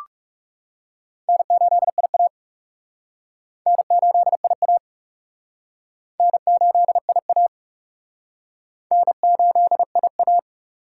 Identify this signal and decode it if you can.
Morse code, heard as a keyed tone, about 38 wpm, 700 Hz. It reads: N8IA N8IA N8IA N8IA